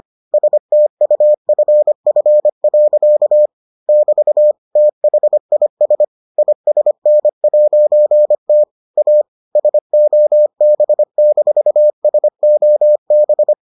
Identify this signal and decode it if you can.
Morse code, heard as a keyed tone, about 25 wpm, 600 Hz. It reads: STUFF. <BT> THIS ISN'T A SOB-SOB